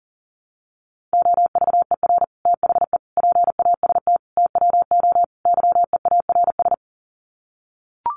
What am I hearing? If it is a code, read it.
Morse code, heard as a keyed tone, about 40 wpm, 700 Hz. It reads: OVER THE PAST TWO YEARS
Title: Sets of 5 Words Encore 40wpm